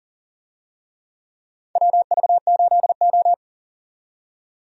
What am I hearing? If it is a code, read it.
Morse code, heard as a keyed tone, about 40 wpm, 700 Hz. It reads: WV8O
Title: USA Call Signs - 4 Characters - 40wpm